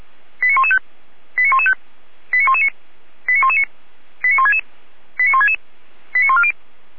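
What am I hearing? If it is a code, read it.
Five-tone selective call (ZVEI), heard as quick runs of five tones, tones of about 70 ms. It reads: 89115 89115 89119 89119 89166 89166 89140
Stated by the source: ZVEI five-tone call received on SDR, archived by feuerwehr-orlamuende (Orlamünde fire brigade).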